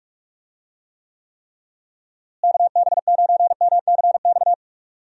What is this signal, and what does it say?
Morse code, heard as a keyed tone, about 43 wpm, 700 Hz. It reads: KB9MCX
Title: USA Call Signs - 6 Characters - Repeated 3x - 45wpm